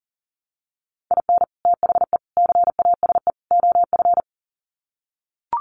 Morse code, heard as a keyed tone, about 37 wpm, 700 Hz. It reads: IN THE CASE OF
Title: Sets of 4 Words 40wpm